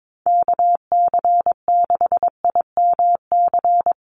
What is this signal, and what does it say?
Morse code, heard as a keyed tone, about 22 wpm, 700 Hz. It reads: KC6IMC
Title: USA Call Signs 22wpm